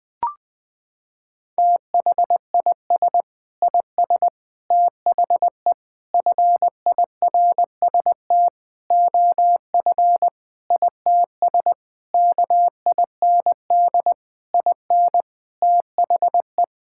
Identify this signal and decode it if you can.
Morse code, heard as a keyed tone, about 20 wpm, 700 Hz. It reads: THIS IS THE FIRST OF ITS KIND IN THE
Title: Sentences with 6-Word Phrases - 20wpm